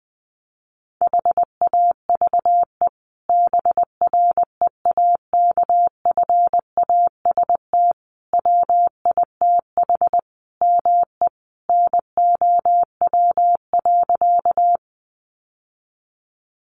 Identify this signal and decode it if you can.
Morse code, heard as a keyed tone, about 20 wpm, 700 Hz. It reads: HAVE BREAKFAST WITH ME NOW.
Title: Sentences from Top 1500 Words 20wpm